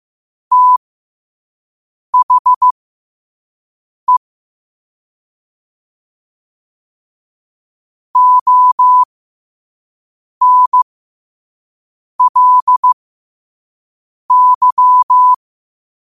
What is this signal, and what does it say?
Morse code, heard as a keyed tone, about 15 wpm, spaced as for 5 wpm, 1000 Hz. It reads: THE ONLY
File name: qotdfiles / 5wpm